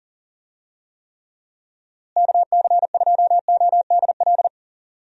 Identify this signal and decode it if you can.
Morse code, heard as a keyed tone, about 40 wpm, 700 Hz. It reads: KC2ODL
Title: USA Call Signs - 6 Characters - Repeated 2x - 40wpm